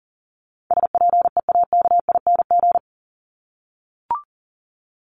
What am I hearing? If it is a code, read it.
Morse code, heard as a keyed tone, about 40 wpm, 700 Hz. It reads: SPEAKING
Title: ING Suffix - Top 100 Words with Base Words 40wpm